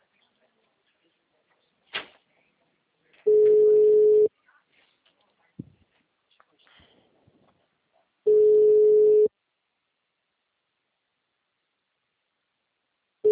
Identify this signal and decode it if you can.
Ringback tone (Europe, 425 Hz), with speech.
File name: out- -2002- - - .26